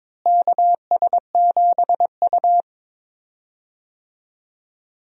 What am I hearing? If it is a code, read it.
Morse code, heard as a keyed tone, about 22 wpm, 700 Hz. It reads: KS7U
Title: USA Call Signs - 4 Characters - Repeated 3x - 22wpm